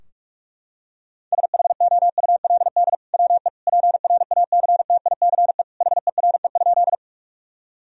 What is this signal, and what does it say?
Morse code, heard as a keyed tone, about 47 wpm, 700 Hz. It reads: SHOULD WE PRACTICE HERE?
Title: Sentences from Top 1000 Words - Review Part 4 - 45wpm